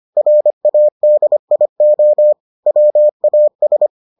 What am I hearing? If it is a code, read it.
Morse code, heard as a keyed tone, about 25 wpm, 600 Hz. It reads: RADIO WAS